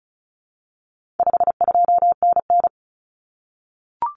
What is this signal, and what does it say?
Morse code, heard as a keyed tone, about 34 wpm, 700 Hz. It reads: 52NN
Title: Common Cut Numbers - 4-Digit 35wpm